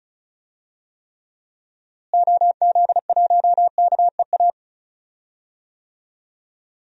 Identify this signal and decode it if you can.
Morse code, heard as a keyed tone, about 34 wpm, 700 Hz. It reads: OZ1KEA
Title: DXCC-100 Call Signs - 6 Characters - Repeated 2x - 35wpm